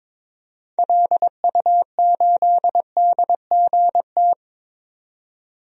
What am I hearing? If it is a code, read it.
Morse code, heard as a keyed tone, about 22 wpm, 700 Hz. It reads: LU8DGT
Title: DXCC-100 Call Signs - 6 Characters - 22wpm